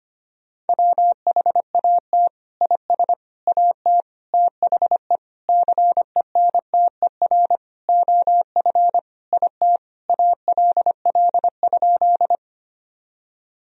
Morse code, heard as a keyed tone, about 25 wpm, 700 Hz. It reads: WHAT IS AT THE CENTER OF IT ALL?
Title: Sentences from Top 1000 Words - Review Part 2 - 25wpm